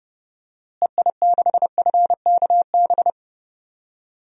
Morse code, heard as a keyed tone, about 30 wpm, 700 Hz. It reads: EI6FKB
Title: DXCC-100 Call Signs - 6 Characters - Repeated 2x - 30wpm